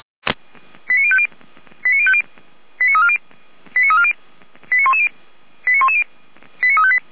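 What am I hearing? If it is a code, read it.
Five-tone selective call (ZVEI), heard as quick runs of five tones, tones of about 80 ms. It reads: 89050 89050 89340 89340 89110 89110 89358
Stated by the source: ZVEI five-tone call 89042 received by SDR.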